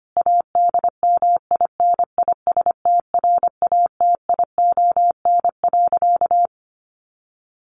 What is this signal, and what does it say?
Morse code, heard as a keyed tone, about 25 wpm, 700 Hz. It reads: ADMINISTRATION.
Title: Sentences from Top 1300 Words 25wpm